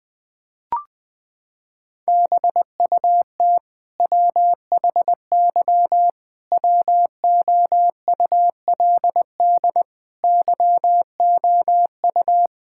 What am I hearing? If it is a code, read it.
Morse code, heard as a keyed tone, about 20 wpm, 700 Hz. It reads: BUT WHY WOULD YOU
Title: Sentences from Top 1000 Words - Review Part 6 - 20wpm